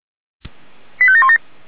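Five-tone selective call (ZVEI), heard as quick runs of five tones, tones of about 70 ms. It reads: 85717
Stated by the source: ZVEI five-tone selective call received by SDR, recorded by Feuerwehr Orlamünde.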